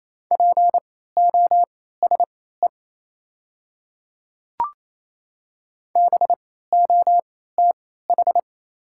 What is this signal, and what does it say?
Morse code, heard as a keyed tone, about 27 wpm, spaced as for 16 wpm, 690 Hz. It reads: POSE BOTH
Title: 3x Character Spacing - Top 500 Words - 28wpm